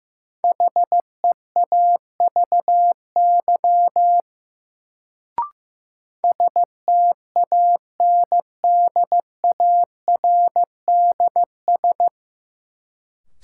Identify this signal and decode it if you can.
Morse code, heard as a keyed tone, about 15 wpm, 700 Hz. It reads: HEAVY STANDARDS
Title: Sentences from Top 1000 Words - Review Part 6 - New Word Drill - 15wpm